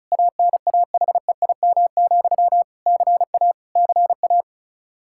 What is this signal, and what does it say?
Morse code, heard as a keyed tone, about 34 wpm, 700 Hz. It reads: ANAHEIM, CA CA
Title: Practice QSO - Exchange 1 - Beginning - 35wpm